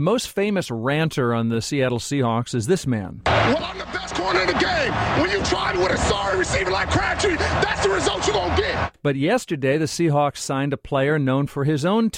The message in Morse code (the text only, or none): none